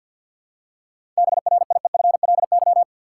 Morse code, heard as a keyed tone, about 48 wpm, 700 Hz. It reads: BRIEFLY
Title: Seven Letter Words - 50wpm